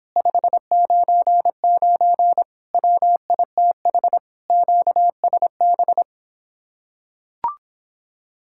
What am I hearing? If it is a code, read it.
Morse code, heard as a keyed tone, about 26 wpm, 700 Hz. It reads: 599 WITH QSB